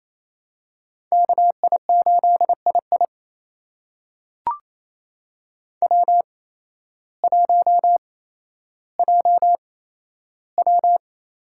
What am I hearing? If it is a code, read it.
Morse code, heard as a keyed tone, about 27 wpm, spaced as for 13 wpm, 700 Hz. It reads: KI8II W1JW
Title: USA Call Signs - 8x Character Spacing - 28wpm